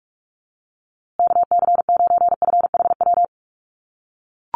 Morse code, heard as a keyed tone, about 44 wpm, 700 Hz. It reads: KC9FHW
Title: USA Call Signs Repeated 3x 45wpm